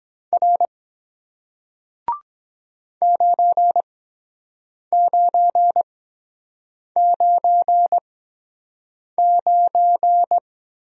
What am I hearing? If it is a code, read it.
Morse code, heard as a keyed tone, about 23 wpm, spaced as for 7 wpm, 700 Hz. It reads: R 9999